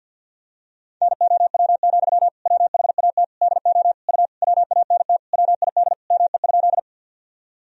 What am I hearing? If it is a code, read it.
Morse code, heard as a keyed tone, about 52 wpm, 700 Hz. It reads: NOW, WHAT DO U WANT WID ME?
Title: Sentences from Top 200 Words and Abbreviations 50wpm